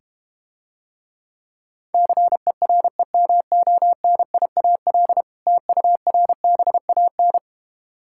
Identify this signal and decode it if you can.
Morse code, heard as a keyed tone, about 32 wpm, 700 Hz. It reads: CEREMONIAL TURBAN